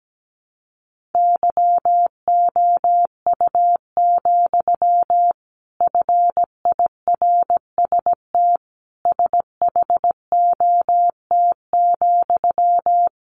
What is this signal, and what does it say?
Morse code, heard as a keyed tone, about 17 wpm, 700 Hz. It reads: YOU, FIRST SHOT,
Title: A Sound Of Thunder 17wpm